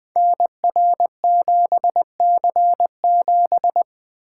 Morse code, heard as a keyed tone, about 20 wpm, 700 Hz. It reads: NR7C7
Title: Code Groups - 5 Letter:Numbers - 20wpm